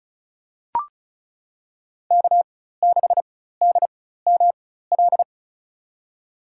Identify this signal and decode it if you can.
Morse code, heard as a keyed tone, about 34 wpm, spaced as for 16 wpm, 700 Hz. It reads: K6DML
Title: USA Call Signs - 4x Character Spacing - 35wpm